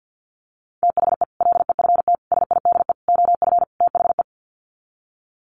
Morse code, heard as a keyed tone, about 50 wpm, 700 Hz. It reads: THE LEFT SIDE OF THE
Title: Sets of 5 Words Encore 50wpm